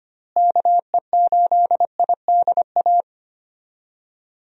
Morse code, heard as a keyed tone, about 25 wpm, 700 Hz. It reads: KE8IDA
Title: USA Call Signs - 6 Characters - Repeated 3x - 25wpm